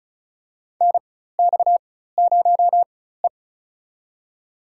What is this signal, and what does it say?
Morse code, heard as a keyed tone, about 34 wpm, spaced as for 16 wpm, 700 Hz. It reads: NX0E
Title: USA Call Signs - 4 Characters - 4x Character Spacing - 35wpm